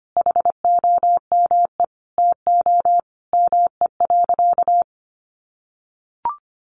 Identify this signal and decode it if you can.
Morse code, heard as a keyed tone, about 25 wpm, 700 Hz. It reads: HOME TO ME.
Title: Sentences from Top 1100 Words 25wpm